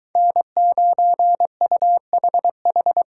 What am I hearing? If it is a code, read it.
Morse code, heard as a keyed tone, about 23 wpm, 700 Hz. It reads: N9UHH